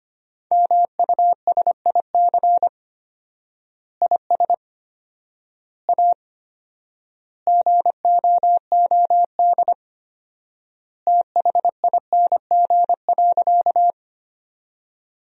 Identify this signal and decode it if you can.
Morse code, heard as a keyed tone, about 25 wpm, 700 Hz. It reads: MUSIC IS A GOOD THING.
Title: 4x Word-Spacing - Sentences from Top 300 Words - 25wpm